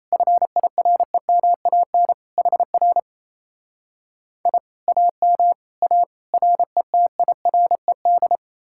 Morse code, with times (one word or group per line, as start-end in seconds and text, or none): 0.1-2.1 FIREMAN
2.4-3.0 HR
4.5-4.6 I
4.9-5.5 AM
5.8-6.0 A
6.3-8.4 RETIRED